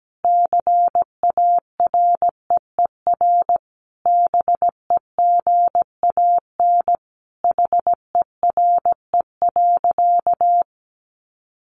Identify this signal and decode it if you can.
Morse code, heard as a keyed tone, about 17 wpm, 700 Hz. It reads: CAREER BEGAN HERE.
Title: Sentences from Top 1100 Words 17wpm